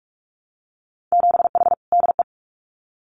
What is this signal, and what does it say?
Morse code, heard as a keyed tone, about 46 wpm, 700 Hz. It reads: 7H DE
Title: Winter Field Day Exchanges Encore 45wpm